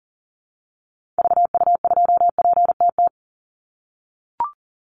Morse code, heard as a keyed tone, about 40 wpm, 700 Hz. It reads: VU2PTT
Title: Call Signs - 40wpm